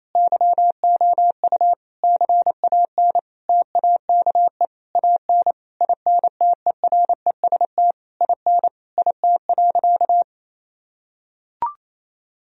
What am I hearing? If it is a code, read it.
Morse code, heard as a keyed tone, about 28 wpm, 700 Hz. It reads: YOU CAN TAKE AN INTEREST IN IT.
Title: Sentences from Top 1000 Words - Review Part 2 - 28wpm